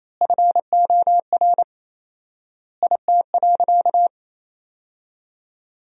Morse code, heard as a keyed tone, about 28 wpm, 700 Hz. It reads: FOR IT.
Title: 4x Word-Spacing - Sentences from Top 300 Words - 28wpm